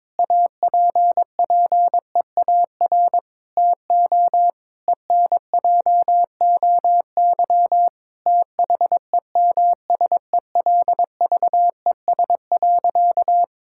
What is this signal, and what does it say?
Morse code, heard as a keyed tone, about 22 wpm, 700 Hz. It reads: APPEAR TO ENJOY THEMSELVES.
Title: Sentences from Top 1000 Words - Review Part 5 - 22wpm